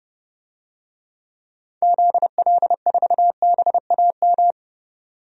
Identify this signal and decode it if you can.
Morse code, heard as a keyed tone, about 32 wpm, 700 Hz. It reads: ZL4BAM
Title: DXCC-100 Call Signs - 6 Characters - 30wpm